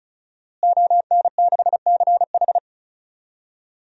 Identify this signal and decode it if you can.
Morse code, heard as a keyed tone, about 36 wpm, 700 Hz. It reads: ON6CH
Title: DXCC-100 Call Signs - 5 Characters - Repeated 3x - 35wpm